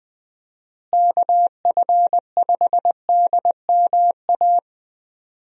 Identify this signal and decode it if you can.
Morse code, heard as a keyed tone, about 20 wpm, 700 Hz. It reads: KF5DMA
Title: USA Call Signs - 6 Characters - Repeated 3x - 20wpm